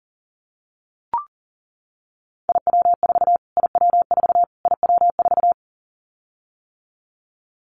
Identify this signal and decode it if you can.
Morse code, heard as a keyed tone, about 40 wpm, 700 Hz. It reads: IW4 IW4 IW4
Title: Callsign Prefixes for Top 25 DXCC - Repeated 3x - 40wpm